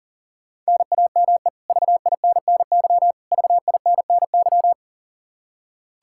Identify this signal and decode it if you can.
Morse code, heard as a keyed tone, about 40 wpm, 700 Hz. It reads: NAME VINNY VINNY
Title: K1USN SST QSO Practice 40wpm